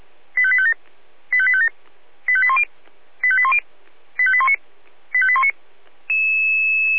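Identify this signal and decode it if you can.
Five-tone selective call (ZVEI), heard as quick runs of five tones, tones of about 70 ms. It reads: 85857 85857 85810 85810 85819 85819
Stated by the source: ZVEI five-tone selective call 85250 received by SDR, from the Feuerwehr Orlamünde audio server.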